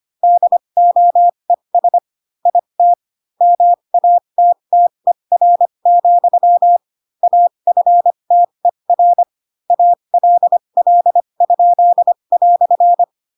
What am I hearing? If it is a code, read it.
Morse code, heard as a keyed tone, about 25 wpm, spaced as for 22 wpm, 700 Hz. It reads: DOES IT MATTER, AFTER ALL?"